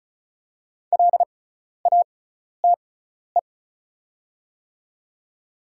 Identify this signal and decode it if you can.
Morse code, heard as a keyed tone, about 36 wpm, spaced as for 12 wpm, 700 Hz. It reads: LATE
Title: 6x Character Spacing - Top 500 Words - 35wpm